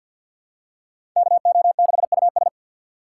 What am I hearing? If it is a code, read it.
Morse code, heard as a keyed tone, about 51 wpm, 700 Hz. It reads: KO6US
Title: USA Call Signs - 2x Character Spacing - 50wpm